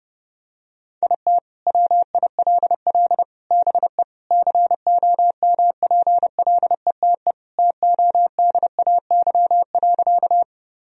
Morse code, heard as a keyed tone, about 32 wpm, 700 Hz. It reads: IT WILL BE COMPLETE TODAY.
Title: Sentences from Top 1000 Words - Review Part 2 - 30wpm